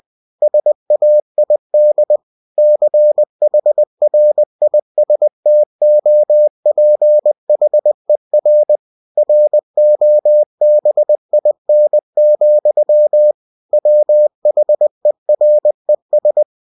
Morse code, heard as a keyed tone, about 20 wpm, 600 Hz. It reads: SAID CHRISTOPHER ROBIN, WHERES
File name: cw-books / winnie-the-pooh / winnie-20wpm-page-0099